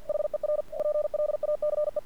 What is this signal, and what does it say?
Morse code, heard as a keyed tone, about 48 wpm, 600 Hz. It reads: HER PLACE
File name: archived / CRSnet